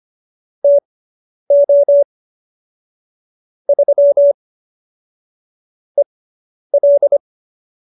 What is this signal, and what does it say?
Morse code, heard as a keyed tone, about 25 wpm, spaced as for 10 wpm, 570 Hz. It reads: TO 3 EL